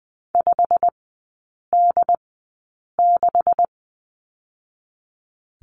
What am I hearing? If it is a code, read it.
Morse code, heard as a keyed tone, about 20 wpm, spaced as for 8 wpm, 700 Hz. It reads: 5D6
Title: 127 Lesson 21e - Number 6 - Three Characters 20wpm